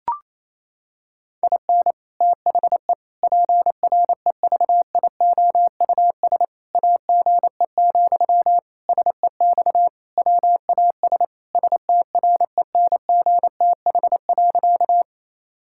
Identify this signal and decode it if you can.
Morse code, heard as a keyed tone, about 28 wpm, 700 Hz. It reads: IN THE PREVIOUS AGE, SEX WAS STRENGTH.
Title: Sentences from Top 1600 Words 28wpm